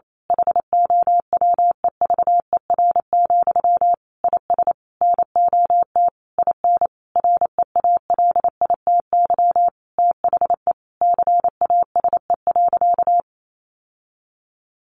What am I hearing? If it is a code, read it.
Morse code, heard as a keyed tone, about 28 wpm, 700 Hz. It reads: HOWEVER, IS NOT IN REALITY THE CASE.
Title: Sentences from Top 1200 Words 28wpm